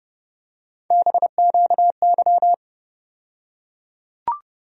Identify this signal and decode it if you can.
Morse code, heard as a keyed tone, about 31 wpm, 700 Hz. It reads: BQY
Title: USA Call Sign Postfixes - 3 Letters 30wpm